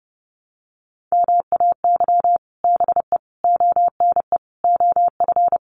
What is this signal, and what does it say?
Morse code, heard as a keyed tone, about 30 wpm, 700 Hz. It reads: MAY BE ONE OF
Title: Sets of 5 Words Encore2 30wpm